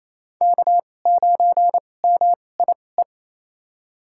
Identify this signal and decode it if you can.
Morse code, heard as a keyed tone, about 28 wpm, spaced as for 20 wpm, 700 Hz. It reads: K9MIE
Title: USA Call Signs - 2x Character Spacing - 28wpm